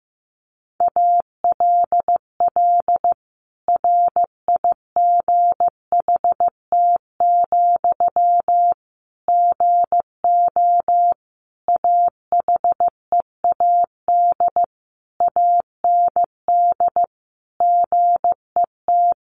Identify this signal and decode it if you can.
Morse code, heard as a keyed tone, about 15 wpm, 700 Hz. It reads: ALL RIGHT, GO AHEAD AND GET
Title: Sentences from Top 1100 Words 15wpm